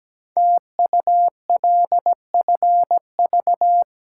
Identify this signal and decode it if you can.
Morse code, heard as a keyed tone, about 17 wpm, 700 Hz. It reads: TULFV
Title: Code Groups - 5 Letters - 17wpm